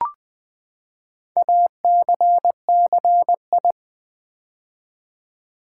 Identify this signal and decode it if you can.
Morse code, heard as a keyed tone, about 20 wpm, 700 Hz. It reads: ACCI
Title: Build A Word - 4 Letters At A Time - 12-Letter Words - 20wpm